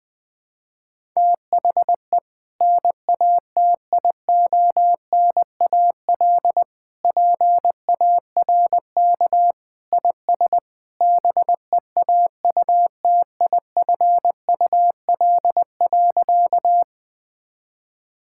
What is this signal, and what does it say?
Morse code, heard as a keyed tone, about 20 wpm, 700 Hz. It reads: THE NATIONAL PARK IS BEAUTIFUL.
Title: Sentences with 2-Word Phrases - 20wpm